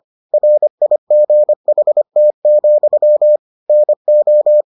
Morse code, heard as a keyed tone, about 25 wpm, 600 Hz. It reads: RIGHT, NO